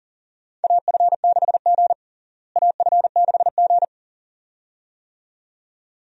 Morse code, heard as a keyed tone, about 40 wpm, 700 Hz. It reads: AF6G AF6G
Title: USA Call Signs - 4 Characters - Repeated 2x - 40wpm